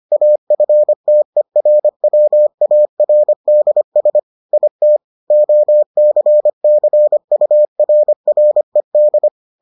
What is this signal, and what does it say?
Morse code, heard as a keyed tone, about 25 wpm, 600 Hz. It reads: AFTERWARDS IT OCCURRED